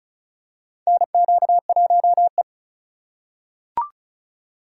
Morse code, heard as a keyed tone, about 34 wpm, 700 Hz. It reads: NQ1E
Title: USA Call Signs - 3x Character Spacing - 35wpm